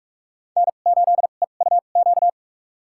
Morse code, heard as a keyed tone, about 46 wpm, spaced as for 32 wpm, 700 Hz. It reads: N8EUQ
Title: USA Call Signs - 5 Characters - 2x Character Spacing - 45wpm